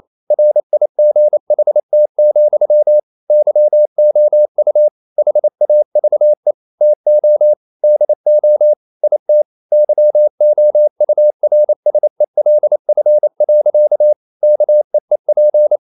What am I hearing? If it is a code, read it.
Morse code, heard as a keyed tone, about 28 wpm, 600 Hz. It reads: RIGHT, YOU HAVE TO DO IT YOURSELF. KEEP
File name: CW / Practice / EnglishSayings28